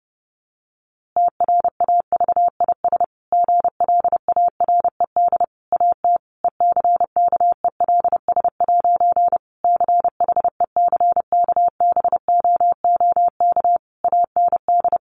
Morse code, heard as a keyed tone, about 30 wpm, 700 Hz. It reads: TRAVIS GLARED AT ECKELS' CHECKBOOK AND
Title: A Sound Of Thunder 30wpm